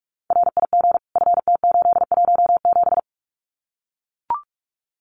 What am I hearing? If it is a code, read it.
Morse code, heard as a keyed tone, about 44 wpm, 700 Hz. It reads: RIG FT817
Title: Common QSO Phrases Encore 45wpm